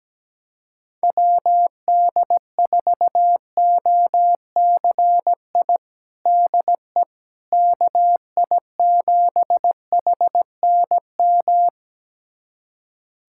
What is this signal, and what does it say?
Morse code, heard as a keyed tone, about 17 wpm, 700 Hz. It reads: WD4OCI DE KI7HNM
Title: USA Callsign DE Callsign - 6 Characters - 17wpm